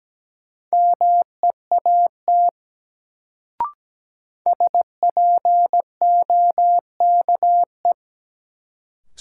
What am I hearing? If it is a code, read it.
Morse code, heard as a keyed tone, about 17 wpm, 700 Hz. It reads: MEAT SPOKE
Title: Sentences from Top 1200 Words - New Word Drill Alt - 17wpm